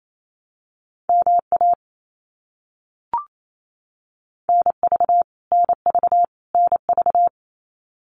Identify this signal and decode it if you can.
Morse code, heard as a keyed tone, about 28 wpm, 700 Hz. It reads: MA NV NV NV
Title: US States Repeated 3x 28wpm